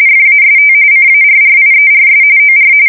RTTY, about 45.45 baud. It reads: FAST CARRIERS OR
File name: fsk170 / USS-New-Jersey